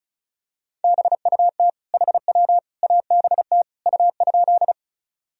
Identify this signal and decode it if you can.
Morse code, heard as a keyed tone, about 36 wpm, 700 Hz. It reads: BUT HW ABT U?
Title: Sentences from Top 100 Words and Abbreviations 35wpm